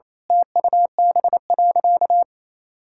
Morse code, heard as a keyed tone, about 27 wpm, 700 Hz. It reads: TUB.
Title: Pangram Challenge v2 - 28wpm